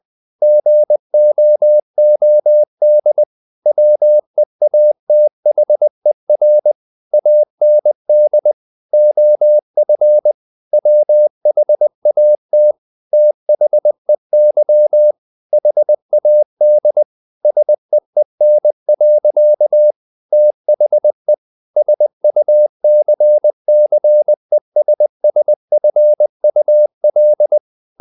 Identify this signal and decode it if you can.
Morse code, heard as a keyed tone, about 20 wpm, 600 Hz. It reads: GOOD WEATHER AND OF WHAT THEY HAD SEEN. THE SUCCESSFUL
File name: the old man and the sea / the old man and the sea part 1 20wpm